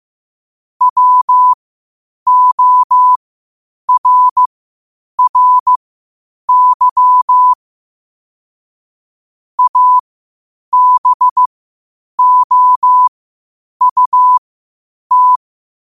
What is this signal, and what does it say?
Morse code, heard as a keyed tone, about 15 wpm, spaced as for 8 wpm, 1000 Hz. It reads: WORRY ABOUT
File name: qotdfiles / 8wpm